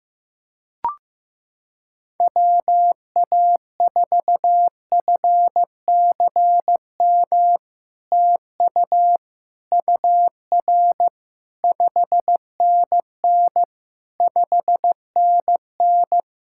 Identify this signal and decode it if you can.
Morse code, heard as a keyed tone, about 15 wpm, 700 Hz. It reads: WA4FCM TU UR 5NN 5NN
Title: SOTA QSO Practice 15wpm